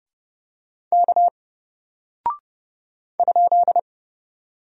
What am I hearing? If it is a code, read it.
Morse code, heard as a keyed tone, about 30 wpm, 700 Hz. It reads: K ?
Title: Single Character (Letter:Number:Punctuation:Prosign) Alt 30wpm